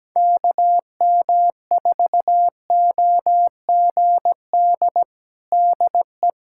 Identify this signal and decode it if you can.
Morse code, heard as a keyed tone, about 17 wpm, 700 Hz. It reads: KM4OGD DE
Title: USA Callsign DE Callsign - 6 Characters - 17wpm